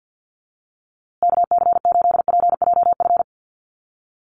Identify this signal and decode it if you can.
Morse code, heard as a keyed tone, about 50 wpm, 700 Hz. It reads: KC8PJF
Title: USA Call Signs Repeated 3x 50wpm